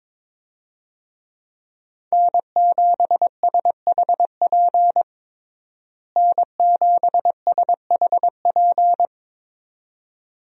Morse code, heard as a keyed tone, about 22 wpm, 700 Hz. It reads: N7SHP N7SHP
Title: USA Call Signs - 5 Characters - Repeated 2x - 22wpm